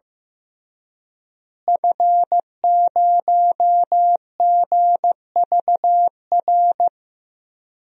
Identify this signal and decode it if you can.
Morse code, heard as a keyed tone, about 15 wpm, 700 Hz. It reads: F0GVR
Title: DXCC-100 Call Signs - 5 Characters - Repeated 3x - 15wpm